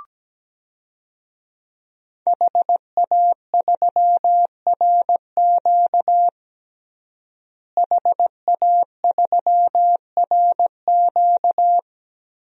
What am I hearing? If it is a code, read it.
Morse code, heard as a keyed tone, about 17 wpm, 700 Hz. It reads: HA3RQ HA3RQ
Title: DXCC-100 Call Signs - 5 Characters - Repeated 2x - 17wpm